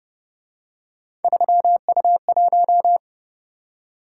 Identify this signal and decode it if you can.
Morse code, heard as a keyed tone, about 28 wpm, 700 Hz. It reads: 3U1